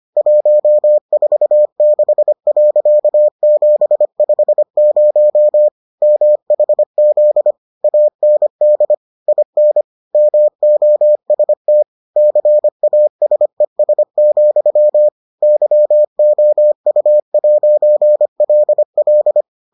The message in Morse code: 146.750 MHZ AND IN MOST CASES, YOU'LL

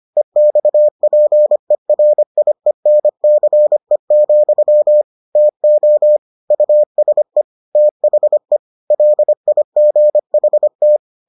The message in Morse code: EXPERIENCE, TO USE THE LIGHT

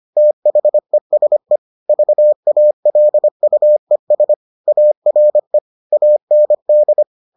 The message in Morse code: THESE VALUES ARE AND

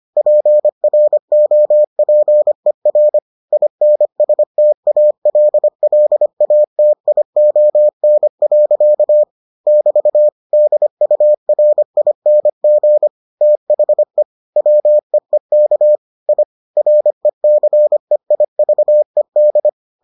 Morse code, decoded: PROPER INSTALLATION. <BT> DURING THE WEEK I RECEIVED

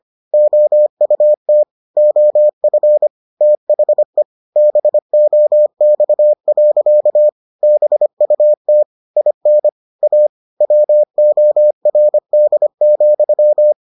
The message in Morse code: OUT OF THE BOX. BUT IN A WORD,

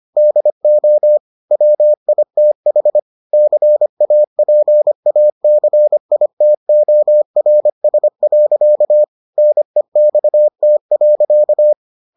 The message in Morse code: DO WITH CAPACITORS. NEXT.

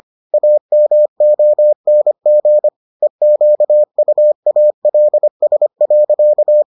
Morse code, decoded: AMONG EQUALS.

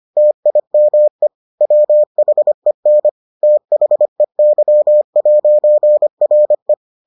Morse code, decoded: TIME WHEN THEY'RE

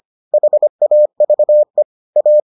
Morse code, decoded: HAVE A

